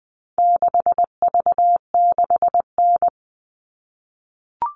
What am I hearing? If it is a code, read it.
Morse code, heard as a keyed tone, about 20 wpm, 700 Hz. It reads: 6V6N
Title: All Cut Numbers - 4-Digit 20wpm